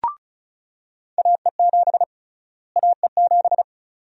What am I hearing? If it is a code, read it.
Morse code, heard as a keyed tone, about 34 wpm, 700 Hz. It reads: AE7 AE7
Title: USA Call Sign Prefixes - Repeated 2x - 35wpm